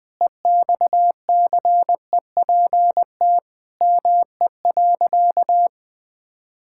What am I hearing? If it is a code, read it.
Morse code, heard as a keyed tone, about 21 wpm, 700 Hz. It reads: EXCEPT ME.
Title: Sentences from Top 1600 Words 20wpm